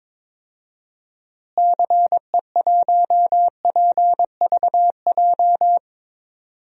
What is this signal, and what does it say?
Morse code, heard as a keyed tone, about 22 wpm, 700 Hz. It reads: CE1PVJ
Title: DXCC-100 Call Signs - 6 Characters - Repeated 3x - 22wpm